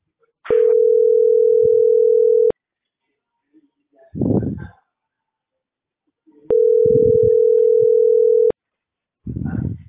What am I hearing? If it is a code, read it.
Ringback tone (North America), 440 + 480 Hz.